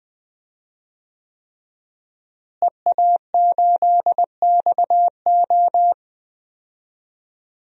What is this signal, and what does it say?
Morse code, heard as a keyed tone, about 21 wpm, 700 Hz. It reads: EA8XO
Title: DXCC-100 Call Signs - 5 Characters - Repeated 3x - 20wpm